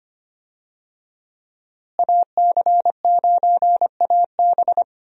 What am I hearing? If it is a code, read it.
Morse code, heard as a keyed tone, about 25 wpm, 690 Hz. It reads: AC9AB
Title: USA Call Signs - 5 Characters - Repeated 2x - 25wpm